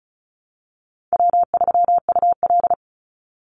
Morse code, heard as a keyed tone, about 34 wpm, 700 Hz. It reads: W3UL
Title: Easy Call Signs Repeated 2x - 35wpm